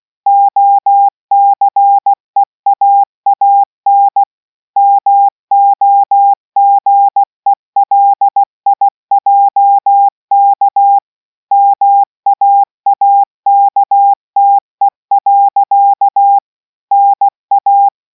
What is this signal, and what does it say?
Morse code, heard as a keyed tone, about 16 wpm, 800 Hz. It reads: OCEAAN MOGELIJK MAAKTE. NA